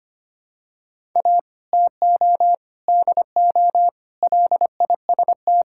Morse code, heard as a keyed tone, about 25 wpm, 700 Hz. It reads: A TO DO LIST
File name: practice / 25wpm / Difficult Sentences with Hints - 25wpm